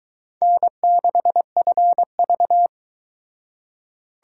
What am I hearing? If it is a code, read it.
Morse code, heard as a keyed tone, about 22 wpm, 700 Hz. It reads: N6FV